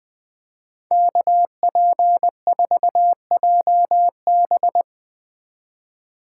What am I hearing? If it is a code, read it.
Morse code, heard as a keyed tone, about 20 wpm, 700 Hz. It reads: KP4JB